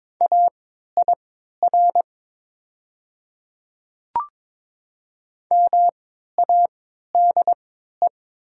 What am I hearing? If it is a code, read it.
Morse code, heard as a keyed tone, about 22 wpm, spaced as for 12 wpm, 700 Hz. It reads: AIR MADE
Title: 3x Character Spacing - Top 300 Words - 22wpm